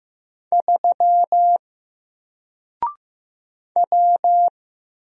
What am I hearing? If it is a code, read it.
Morse code, heard as a keyed tone, about 15 wpm, 700 Hz. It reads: 3 W
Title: Single Character (Letter:Number:Punctuation:Prosign) 15wpm